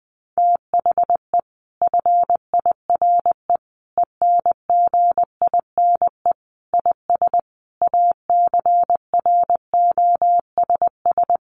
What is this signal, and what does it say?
Morse code, heard as a keyed tone, about 20 wpm, 700 Hz. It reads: THE FIRE ENGINE IS ACROSS
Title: Sentences from Top 1500 Words 20wpm